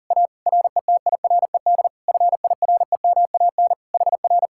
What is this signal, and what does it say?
Morse code, heard as a keyed tone, about 52 wpm, 700 Hz. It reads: A RETIRED FIREMAN HR